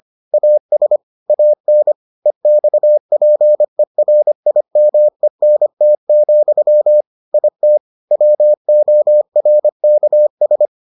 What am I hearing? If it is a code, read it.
Morse code, heard as a keyed tone, about 25 wpm, 600 Hz. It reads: AS AN EXPERIMENT, IT WORKS